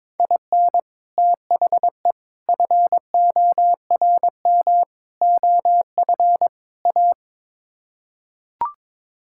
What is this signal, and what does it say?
Morse code, heard as a keyed tone, about 23 wpm, 700 Hz. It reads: IN THE FORM OF A
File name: practice / 22wpm / Build A Phrase - 5-Word Phrases - 22wpm